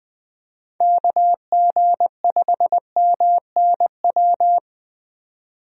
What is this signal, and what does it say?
Morse code, heard as a keyed tone, about 20 wpm, 700 Hz. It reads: KG5MNW
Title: USA Call Signs - 6 Characters - 20wpm